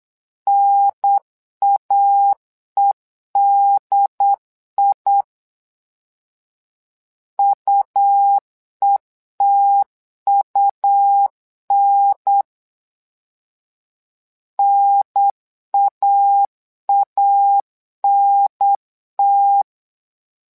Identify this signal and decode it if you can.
Morse code, heard as a keyed tone, about 9 wpm, 800 Hz. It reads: NAEDI UETUN NAANT
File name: LEZIONI / 2 UHDN / 2 UHDN 8wpm2232